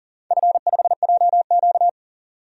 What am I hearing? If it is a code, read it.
Morse code, heard as a keyed tone, about 38 wpm, 700 Hz. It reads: F5JQ
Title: DXCC-100 Call Signs - 4 Characters - 40wpm